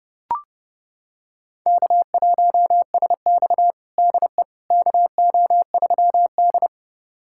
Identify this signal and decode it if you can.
Morse code, heard as a keyed tone, about 30 wpm, 700 Hz. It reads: K1SX DE KO3D
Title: USA Callsign DE Callsign - 4 Characters - 30wpm